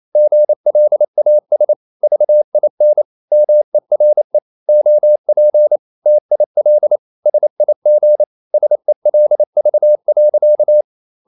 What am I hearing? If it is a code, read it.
Morse code, heard as a keyed tone, about 28 wpm, 600 Hz. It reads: GLAS VIN MERE OP TIL SIG SELV.
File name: QRQ 28wpm